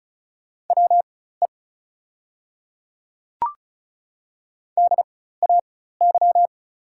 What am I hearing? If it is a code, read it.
Morse code, heard as a keyed tone, about 36 wpm, spaced as for 16 wpm, 700 Hz. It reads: WE DAY
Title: 4x Character Spacing - Top 100 Words - 35wpm